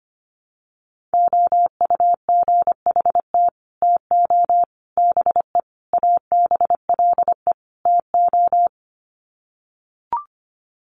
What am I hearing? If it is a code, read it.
Morse code, heard as a keyed tone, about 25 wpm, 700 Hz. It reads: OUGHT TO BE ABLE TO
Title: Sets of 5 Words Encore2 25wpm